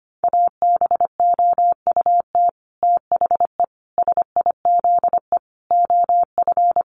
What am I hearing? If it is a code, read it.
Morse code, heard as a keyed tone, about 25 wpm, 700 Hz. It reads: ABOUT THE SIZE OF
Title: Sets of 5 Words 25wpm